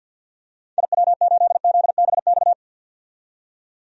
Morse code, heard as a keyed tone, about 53 wpm, 700 Hz. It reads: IW8ZBX
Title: DXCC-100 Call Signs - 6 Characters - 50wpm